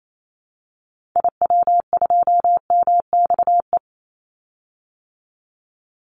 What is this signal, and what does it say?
Morse code, heard as a keyed tone, about 28 wpm, 700 Hz. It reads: IW2MXE
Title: Call Signs Repeated 2x - 28wpm